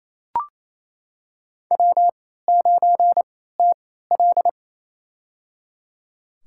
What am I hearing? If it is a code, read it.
Morse code, heard as a keyed tone, about 28 wpm, spaced as for 16 wpm, 700 Hz. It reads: W9TL
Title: USA Call Signs - 3x Character Spacing - 28wpm